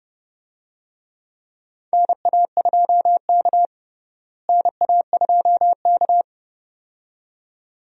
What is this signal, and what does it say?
Morse code, heard as a keyed tone, about 32 wpm, 700 Hz. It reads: NA2K NA2K
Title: USA Call Signs - 4 Characters - Repeated 2x - 30wpm